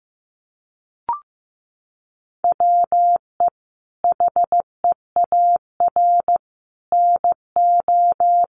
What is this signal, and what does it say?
Morse code, heard as a keyed tone, about 15 wpm, 700 Hz. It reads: WE HEAR NO
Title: Sentences from Top 1400 Words 15wpm